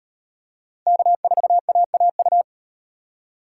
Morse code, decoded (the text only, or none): K4AAU